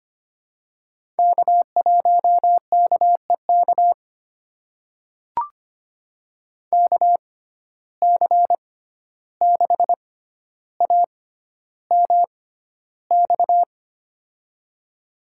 Morse code, heard as a keyed tone, about 25 wpm, spaced as for 12 wpm, 700 Hz. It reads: K1KEK KC6AMX
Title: USA Call Signs - 6x Character Spacing - 25wpm